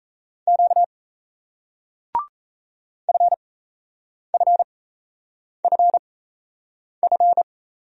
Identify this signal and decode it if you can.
Morse code, heard as a keyed tone, about 38 wpm, spaced as for 8 wpm, 700 Hz. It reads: Q FFFF